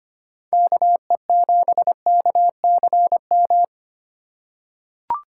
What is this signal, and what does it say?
Morse code, heard as a keyed tone, about 25 wpm, 700 Hz. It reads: KE7KCM
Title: USA Call Signs - 3x Character Spacing - 25wpm